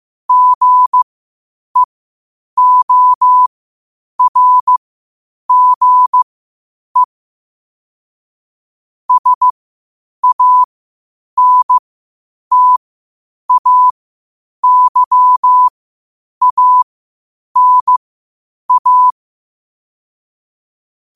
Morse code, heard as a keyed tone, about 15 wpm, spaced as for 8 wpm, 1000 Hz. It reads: GEORGE SANTAYANA
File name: qotdfiles / 8wpm